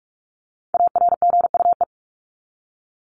Morse code, heard as a keyed tone, about 44 wpm, 700 Hz. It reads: ARGUE
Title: Top 1501-1750 Words 45wpm